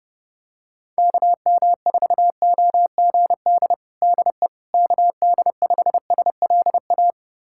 Morse code, heard as a keyed tone, about 29 wpm, 700 Hz. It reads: KM4OGD DE KD5SLA
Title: USA Callsign DE Callsign - 6 Characters - 30wpm